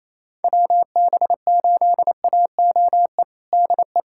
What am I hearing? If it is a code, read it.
Morse code, heard as a keyed tone, about 27 wpm, 700 Hz. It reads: WB8AOE DE
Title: USA Callsign DE Callsign - 6 Characters - 28wpm